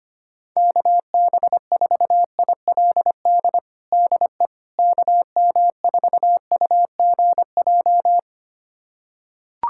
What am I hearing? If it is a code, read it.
Morse code, heard as a keyed tone, about 25 wpm, 700 Hz. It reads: KB4ILD DE KM4UGJ
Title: USA Callsign DE Callsign - 6 Characters - 25wpm